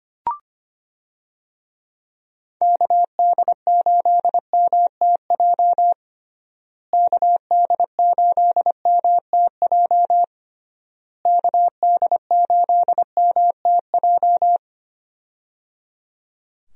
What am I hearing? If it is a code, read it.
Morse code, heard as a keyed tone, about 25 wpm, 700 Hz. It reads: KD8MTJ KD8MTJ KD8MTJ
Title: USA Call Signs - 6 Characters - Repeated 3x - 25wpm